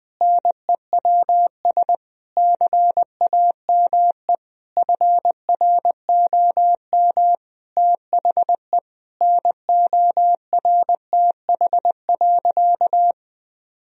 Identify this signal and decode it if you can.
Morse code, heard as a keyed tone, about 20 wpm, 700 Hz. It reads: NEWS CAME FROM THE NORTH.
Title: Sentences from Top 1000 Words - Review Part 1 - 20wpm